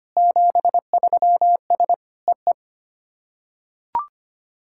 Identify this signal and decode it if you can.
Morse code, heard as a keyed tone, about 25 wpm, 700 Hz. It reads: 73S EE
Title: SOTA QSO Practice 25wpm